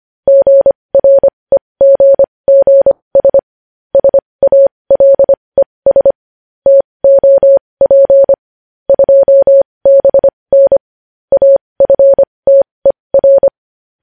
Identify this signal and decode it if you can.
Morse code, heard as a keyed tone, about 25 wpm, spaced as for 20 wpm, 550 Hz. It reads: GREGGS SALES TOP 2BN AFTER
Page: news-headlines-in-morse-code-at-20-wpm